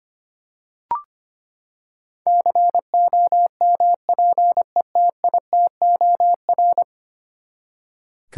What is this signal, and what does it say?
Morse code, heard as a keyed tone, about 25 wpm, 700 Hz. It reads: COMPETITOR
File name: Ten Letter Words - 25wpm